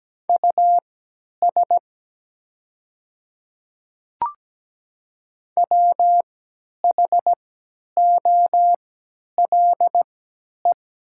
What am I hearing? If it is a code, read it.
Morse code, heard as a keyed tone, about 17 wpm, spaced as for 10 wpm, 700 Hz. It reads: US WHOLE
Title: 3x Character Spacing - Top 500 Words - 17wpm